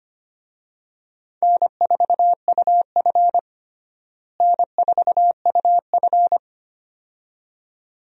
Morse code, heard as a keyed tone, about 25 wpm, 700 Hz. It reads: N4UF N4UF
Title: USA Call Signs - 4 Characters - Repeated 2x - 25wpm